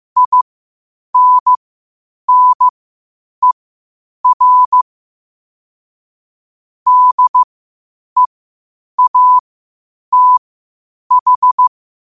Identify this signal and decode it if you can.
Morse code, heard as a keyed tone, about 15 wpm, spaced as for 8 wpm, 1000 Hz. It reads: INNER DEATH